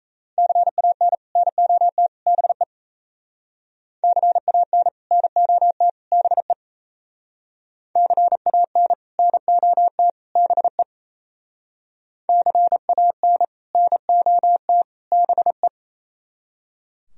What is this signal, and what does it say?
Morse code, heard as a keyed tone, about 33 wpm, 700 Hz. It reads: CAN NOT BE CAN NOT BE CAN NOT BE CAN NOT BE
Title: Speed-Racing - Sets of 3 Words 28wpm